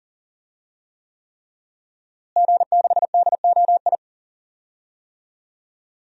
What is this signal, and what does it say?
Morse code, heard as a keyed tone, about 37 wpm, 700 Hz. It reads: G6DOI
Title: DXCC-100 Call Signs - 5 Characters - Repeated 3x - 40wpm